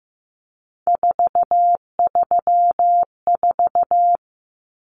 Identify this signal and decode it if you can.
Morse code, heard as a keyed tone, about 15 wpm, 700 Hz. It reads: 434
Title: Common Cut Numbers - 3-Digit 15wpm